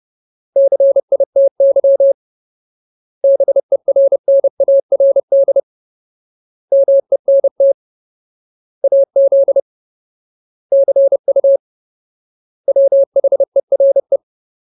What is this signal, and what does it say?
Morse code, heard as a keyed tone, about 30 wpm, 550 Hz. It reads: CITY BERNARD MENT AZ CU WHERE
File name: Ultimate Word List / 30WPM Ultimate